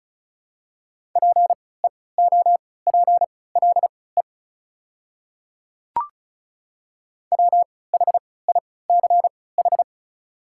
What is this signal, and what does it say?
Morse code, heard as a keyed tone, about 36 wpm, spaced as for 20 wpm, 690 Hz. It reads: PEOPLE WHICH
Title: 3x Character Spacing - Top 100 Words - 35wpm